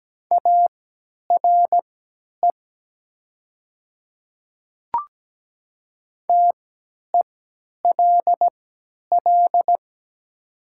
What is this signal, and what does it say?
Morse code, heard as a keyed tone, about 17 wpm, spaced as for 10 wpm, 700 Hz. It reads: ARE TELL
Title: 3x Character Spacing - Top 300 Words - 17wpm